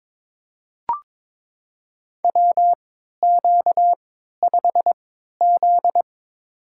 Morse code, heard as a keyed tone, about 22 wpm, spaced as for 13 wpm, 700 Hz. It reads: WQ5Z
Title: USA Call Signs - 4 Characters - 3x Character Spacing - 22wpm